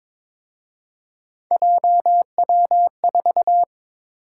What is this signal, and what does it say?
Morse code, heard as a keyed tone, about 26 wpm, 700 Hz. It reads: JW4
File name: Speed-Racing - Three Letter-Number Combinations - 22wpm